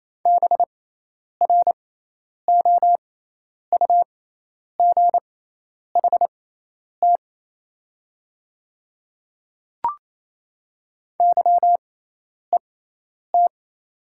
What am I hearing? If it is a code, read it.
Morse code, heard as a keyed tone, about 27 wpm, spaced as for 10 wpm, 700 Hz. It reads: BROUGHT YET
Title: 6x Character Spacing - Top 500 Words - 28wpm